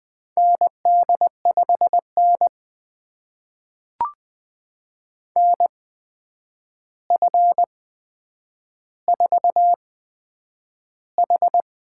Morse code, heard as a keyed tone, about 20 wpm, spaced as for 9 wpm, 700 Hz. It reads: ND5N NF4H
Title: USA Call Signs - 4 Characters - 8x Character Spacing - 20wpm